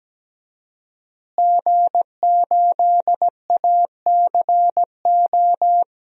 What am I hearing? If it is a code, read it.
Morse code, heard as a keyed tone, about 17 wpm, 700 Hz. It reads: G8ACO
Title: DXCC-100 Call Signs - 5 Characters - Repeated 3x - 17wpm